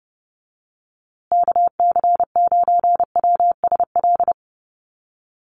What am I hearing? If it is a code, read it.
Morse code, heard as a keyed tone, about 30 wpm, 700 Hz. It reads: KC9WSL
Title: USA Call Signs Repeated 2x 30wpm